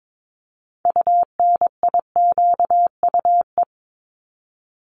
Morse code, heard as a keyed tone, about 22 wpm, 700 Hz. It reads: UNIQUE
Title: Multi-letter Phonemes 22wpm